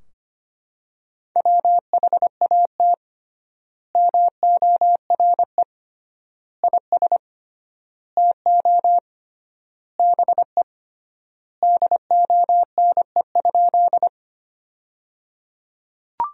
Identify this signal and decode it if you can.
Morse code, heard as a keyed tone, about 25 wpm, 700 Hz. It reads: WHAT MORE IS TO BE DONE?
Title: 3x Word-Spacing - Sentences from Top 500 Words - 25wpm